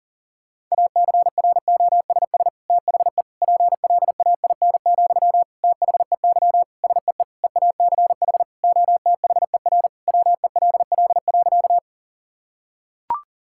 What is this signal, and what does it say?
Morse code, heard as a keyed tone, about 40 wpm, 690 Hz. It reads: ACROSS THE PLAIN, THEY SEE EACH OTHER WELL.
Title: Sentences from Top 1000 Words - Review Part 5 - 40wpm